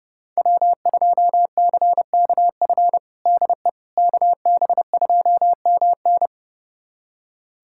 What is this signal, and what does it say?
Morse code, heard as a keyed tone, about 30 wpm, 700 Hz. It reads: W2CKF DE KB2MN
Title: USA Callsign DE Callsign - 5 Characters - 30wpm